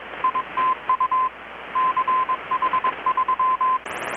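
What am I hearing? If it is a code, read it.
Morse code, heard as a keyed tone, about 21 wpm, 1030 Hz, with radio noise. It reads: ITU CH3
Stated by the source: Morse from KPH's Night of Nights XX maritime radiotelegraphy event, received in Chiapas, Mexico about 3,800 km away on a Kenwood R-600 with a grounded 10 m long-wire antenna.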